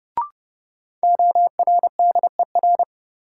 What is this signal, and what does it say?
Morse code, heard as a keyed tone, about 32 wpm, 700 Hz. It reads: ORDER
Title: Sets of 2 Words - New Word Drill - 30wpm